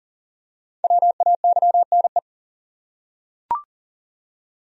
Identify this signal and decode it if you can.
Morse code, heard as a keyed tone, about 40 wpm, 700 Hz. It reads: WAYNE
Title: USA Common Names - Max 5 Letters - 40wpm